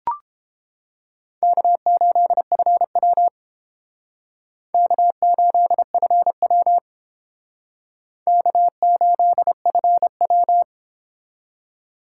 Morse code, heard as a keyed tone, about 30 wpm, 700 Hz. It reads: K8FW K8FW K8FW